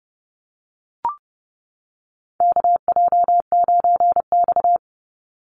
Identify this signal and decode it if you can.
Morse code, heard as a keyed tone, about 30 wpm, 700 Hz. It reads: KJ9X
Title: NA SOTA Chaser Callsigns 30wpm